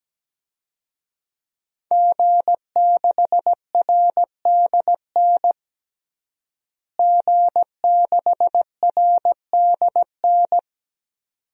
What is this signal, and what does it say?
Morse code, heard as a keyed tone, about 17 wpm, 700 Hz. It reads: G6RDN G6RDN
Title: DXCC-100 Call Signs - 5 Characters - Repeated 2x - 17wpm